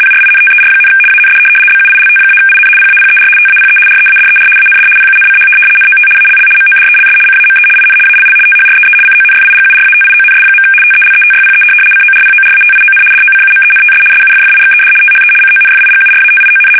RTTY, about 75 baud. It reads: PRINTER, AND BUILT-IN PAPER TAPE READER AND PUNCH NOT ALL MODELS CAME IN ALL THREE CONFIGURATIONS. TELETYPE CORPORATION DOCUMENTS SUFFIXED THE CONFIGURATION
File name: fsk850-75B / teletype-corp